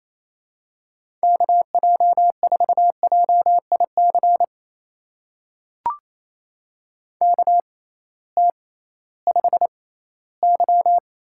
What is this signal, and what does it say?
Morse code, heard as a keyed tone, about 28 wpm, spaced as for 16 wpm, 700 Hz. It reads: KJ4JIC KT5Y
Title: USA Call Signs - 6x Character Spacing - 28wpm